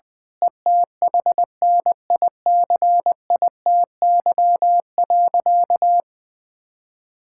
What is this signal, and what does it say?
Morse code, heard as a keyed tone, about 21 wpm, 700 Hz. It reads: ETHNICITY.
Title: Sentences with 9-Word Phrases - Spoken Beginning - 20wpm